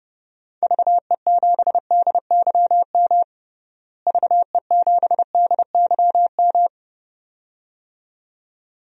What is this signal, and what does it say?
Morse code, heard as a keyed tone, about 29 wpm, 700 Hz. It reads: VE7DYM VE7DYM
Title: DXCC-100 Call Signs - 6 Characters - Repeated 2x - 30wpm